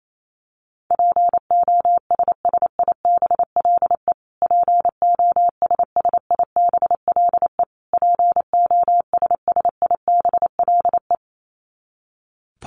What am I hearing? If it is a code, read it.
Morse code, heard as a keyed tone, about 28 wpm, 700 Hz. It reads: POSSIBLE POSSIBLE POSSIBLE
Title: Top 500 Words Repeated 28wpm